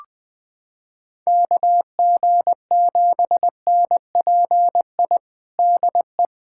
Morse code, heard as a keyed tone, about 20 wpm, 700 Hz. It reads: KG7NPI DE
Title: USA Callsign DE Callsign - 6 Characters - 20wpm